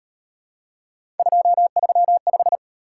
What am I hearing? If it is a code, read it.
Morse code, heard as a keyed tone, about 37 wpm, 700 Hz. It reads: 235